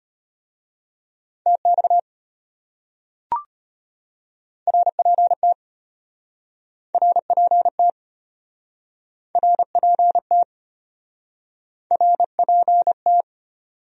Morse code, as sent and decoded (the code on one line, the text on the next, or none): - -..- / .-. .--. - / .-. .--. - / .-. .--. - / .-. .--. -
TX RPT RPT RPT RPT